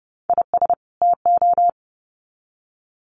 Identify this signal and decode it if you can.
Morse code, heard as a keyed tone, about 30 wpm, 700 Hz. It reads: IS TO
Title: Sets of 2 Words 30wpm